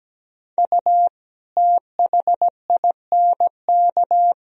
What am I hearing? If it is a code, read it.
Morse code, heard as a keyed tone, about 17 wpm, 700 Hz. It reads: U THINK
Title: Sentences from Top 200 Words and Abbreviations 17wpm